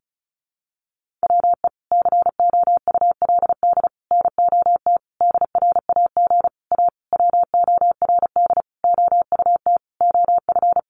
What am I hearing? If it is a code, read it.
Morse code, heard as a keyed tone, about 34 wpm, 700 Hz. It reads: WE COULD NOT DRAG A WORD OUT OF
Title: Sentences from Top 1500 Words 35wpm